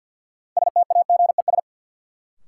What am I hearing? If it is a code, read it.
Morse code, heard as a keyed tone, about 53 wpm, 700 Hz. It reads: STAGES
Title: Sets of 5 Words Encore - New Word Drill - 50wpm